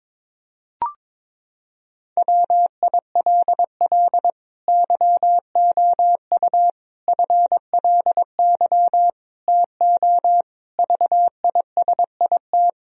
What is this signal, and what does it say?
Morse code, heard as a keyed tone, about 22 wpm, 700 Hz. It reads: WILL YOU FLY TO VISIT
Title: Sentences from Top 1000 Words - Review Part 2 - 22wpm